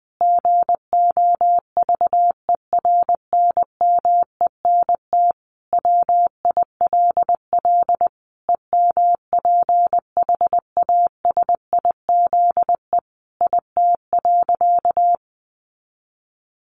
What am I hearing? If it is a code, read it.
Morse code, heard as a keyed tone, about 19 wpm, 700 Hz. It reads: GOVERNMENT WILL EMPHASIZE IT.
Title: Sentences from Top 1500 Words 20wpm